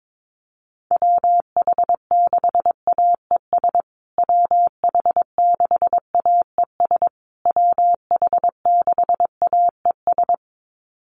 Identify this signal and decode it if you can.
Morse code, heard as a keyed tone, about 22 wpm, 700 Hz. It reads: WH6AES WH6AES WH6AES
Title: USA Call Signs Repeated 3x 22wpm